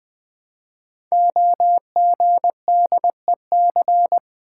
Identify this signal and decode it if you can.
Morse code, heard as a keyed tone, about 20 wpm, 700 Hz. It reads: OGDEC
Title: Code Groups - 5 Letters - 20wpm